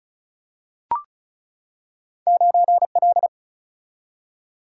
Morse code, heard as a keyed tone, about 35 wpm, 700 Hz. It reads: 9L
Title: Single-Letter Single-Number 35wpm